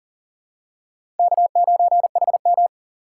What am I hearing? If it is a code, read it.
Morse code, heard as a keyed tone, about 37 wpm, 700 Hz. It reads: K9HM
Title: USA Call Signs - 4 Characters - 4x Character Spacing - 40wpm